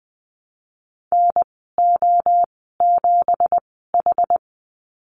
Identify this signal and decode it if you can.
Morse code, heard as a keyed tone, about 20 wpm, spaced as for 15 wpm, 700 Hz. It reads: NO7H
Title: 173 Lesson 28g - Number 9 - Call Signs 20wpm